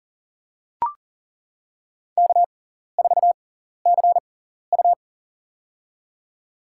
Morse code, heard as a keyed tone, about 39 wpm, spaced as for 14 wpm, 700 Hz. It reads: K4CU
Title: USA Call Signs - 6x Character Spacing - 40wpm